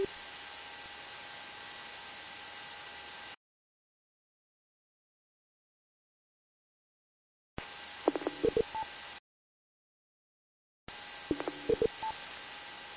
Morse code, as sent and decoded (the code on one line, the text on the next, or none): none